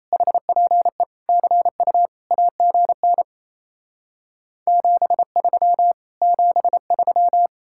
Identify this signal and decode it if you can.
Morse code, heard as a keyed tone, about 37 wpm, 700 Hz. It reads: HPE CU AGN 73 73